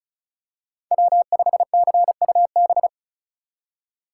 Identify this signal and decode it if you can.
Morse code, heard as a keyed tone, about 36 wpm, 700 Hz. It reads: W5CUB
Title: USA Call Signs - 5 Characters - 2x Character Spacing - 35wpm